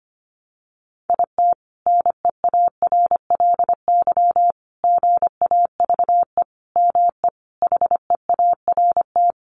Morse code, heard as a keyed tone, about 25 wpm, 700 Hz. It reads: IT NEARLY GAVE ME HEART
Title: Sentences from Top 1300 Words 25wpm